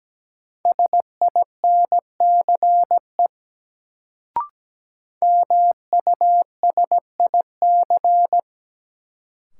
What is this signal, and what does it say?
Morse code, heard as a keyed tone, about 17 wpm, 700 Hz. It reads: SINCE MUSIC
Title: Sentences from Top 300 Words - New Word Drill - 17wpm